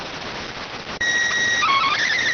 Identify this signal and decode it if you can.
SSTV (slow-scan television), PD 290.